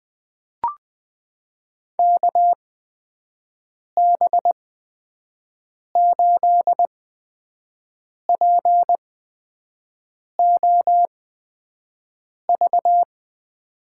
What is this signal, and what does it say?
Morse code, heard as a keyed tone, about 20 wpm, spaced as for 5 wpm, 700 Hz. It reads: KB8POV
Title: USA Call Signs - 8x Character Spacing - 20wpm